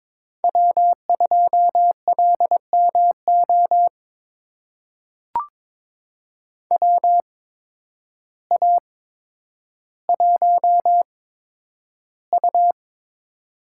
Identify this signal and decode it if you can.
Morse code, heard as a keyed tone, about 21 wpm, spaced as for 10 wpm, 700 Hz. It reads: W2LMO WA1U
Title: USA Call Signs - 8x Character Spacing - 22wpm